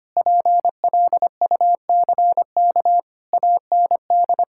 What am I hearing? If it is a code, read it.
Morse code, heard as a keyed tone, about 25 wpm, 700 Hz. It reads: PLUCK AND
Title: Pangram Challenge v2 - 25wpm